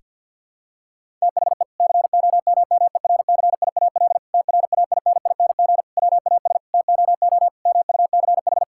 Morse code, heard as a keyed tone, about 51 wpm, 690 Hz. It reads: THE COMMERCIAL TRAINING WAS TOO MUCH
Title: Sentences from Top 1000 Words - Review Part 5 - 50wpm